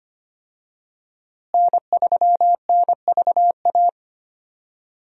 Morse code, heard as a keyed tone, about 25 wpm, 700 Hz. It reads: N3NVA
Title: USA Call Signs - 5 Characters - Repeated 3x - 25wpm